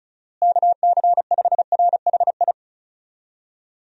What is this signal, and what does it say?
Morse code, heard as a keyed tone, about 36 wpm, 700 Hz. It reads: KC5RHI
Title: USA Call Signs - 6 Characters - Repeated 3x - 35wpm